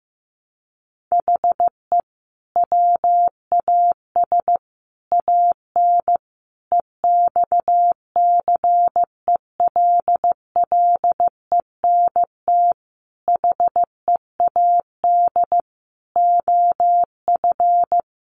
Morse code, heard as a keyed tone, about 15 wpm, 700 Hz. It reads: HE WAS AN EXCELLENT HEAD OF
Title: Sentences from Top 1200 Words 15wpm